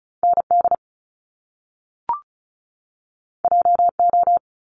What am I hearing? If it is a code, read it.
Morse code, heard as a keyed tone, about 36 wpm, 700 Hz. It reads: ND JO
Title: Top Two Letter Combinations in Words 35wpm